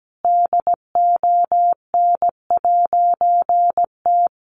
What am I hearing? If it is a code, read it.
Morse code, heard as a keyed tone, about 17 wpm, 700 Hz. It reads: DON'T
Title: A Sound Of Thunder 17wpm